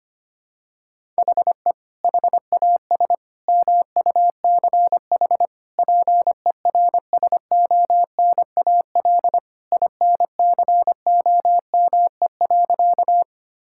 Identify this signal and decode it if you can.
Morse code, heard as a keyed tone, about 25 wpm, 700 Hz. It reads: HE HAS MUCH PERSONAL INCOME.
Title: Sentences from Top 1000 Words - Review Part 5 - 25wpm